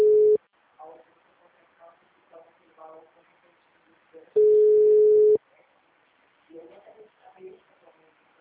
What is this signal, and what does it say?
Ringback tone (Europe, 425 Hz), with speech.